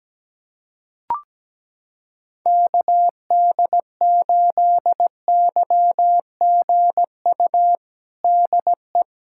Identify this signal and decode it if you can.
Morse code, heard as a keyed tone, about 17 wpm, 700 Hz. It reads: KD8YGU DE
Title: USA Callsign DE Callsign - 6 Characters - 17wpm